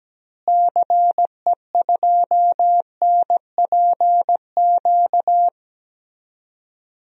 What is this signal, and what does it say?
Morse code, heard as a keyed tone, about 17 wpm, 700 Hz. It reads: CE2NPQ
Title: DXCC-100 Call Signs - 6 Characters - Repeated 3x - 17wpm